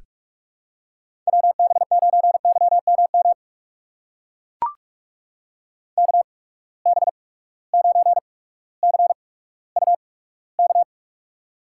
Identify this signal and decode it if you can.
Morse code, heard as a keyed tone, about 44 wpm, spaced as for 19 wpm, 700 Hz. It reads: WB9YMM KB9CUK
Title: USA Call Signs - 6 Characters - 8x Character Spacing - 45wpm